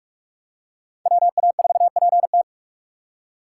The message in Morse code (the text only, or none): WA4PT